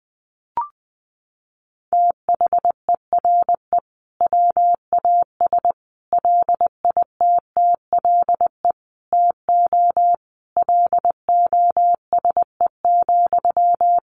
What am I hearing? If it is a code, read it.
Morse code, heard as a keyed tone, about 20 wpm, 700 Hz. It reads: THERE WAS LITTLE TO LOSE,
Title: Sentences from Top 1100 Words 20wpm